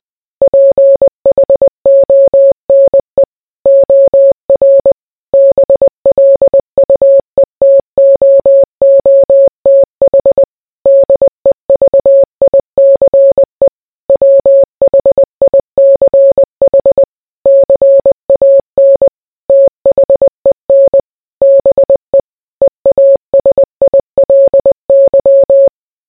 Morse code, heard as a keyed tone, about 20 wpm, 550 Hz. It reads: PHONE OR BLUETOOTH DEVICE WHICH CAN THEN BE EASILY